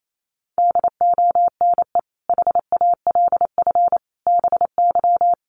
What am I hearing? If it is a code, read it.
Morse code, heard as a keyed tone, about 28 wpm, 700 Hz. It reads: DONE HALF BY
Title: Sentences from Top 800 Words 28wpm